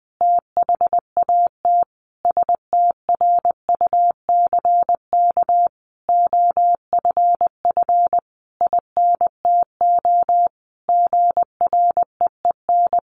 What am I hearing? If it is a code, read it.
Morse code, heard as a keyed tone, about 20 wpm, 700 Hz. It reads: THAT STRUCK OFF INTO GREEN
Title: A Sound Of Thunder 20wpm